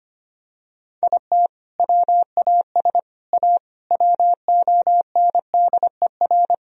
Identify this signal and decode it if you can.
Morse code, heard as a keyed tone, about 25 wpm, 700 Hz. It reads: IT WAS A WONDER
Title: Sentences from Top 1000 Words - Review Part 3 - 25wpm